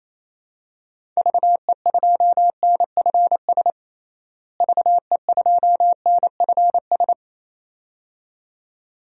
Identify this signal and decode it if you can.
Morse code, heard as a keyed tone, about 28 wpm, 700 Hz. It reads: VE2NFS VE2NFS
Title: DXCC-100 Call Signs - 6 Characters - Repeated 2x - 28wpm